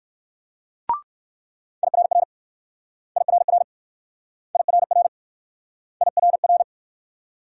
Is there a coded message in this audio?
Yes: Morse code from 1.8 to 6.6 s, heard as a keyed tone, about 62 wpm, 690 Hz. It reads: IRR IRR IRR IRR